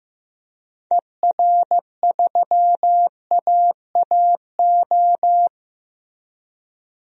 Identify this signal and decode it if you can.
Morse code, heard as a keyed tone, about 15 wpm, 700 Hz. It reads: ER3AAO
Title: DXCC-100 Call Signs - 6 Characters - 15wpm